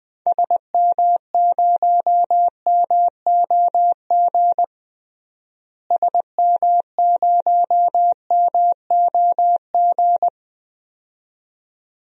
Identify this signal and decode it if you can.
Morse code, heard as a keyed tone, about 20 wpm, 700 Hz. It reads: SM0MOG SM0MOG
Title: DXCC-100 Call Signs - 6 Characters - Repeated 2x - 20wpm